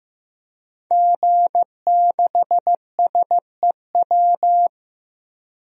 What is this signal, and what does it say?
Morse code, heard as a keyed tone, about 15 wpm, 700 Hz. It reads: G6SEW
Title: DXCC-100 Call Signs - 5 Characters - Repeated 2x - 15wpm